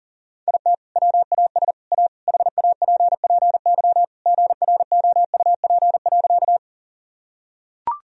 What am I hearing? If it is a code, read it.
Morse code, heard as a keyed tone, about 40 wpm, 700 Hz. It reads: IT WAS A HAPPY GROUP.
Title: Sentences from Top 1000 Words - Review Part 1 - 40wpm